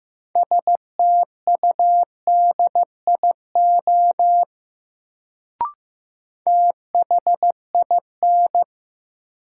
Morse code, heard as a keyed tone, about 15 wpm, 700 Hz. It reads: STUDIO THIN
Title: Sentences from Top 1200 Words - New Word Drill Alt - 15wpm